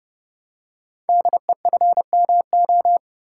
Morse code, heard as a keyed tone, about 30 wpm, 700 Hz. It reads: DEFMO
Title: Code Groups - 5 Letters - 30wpm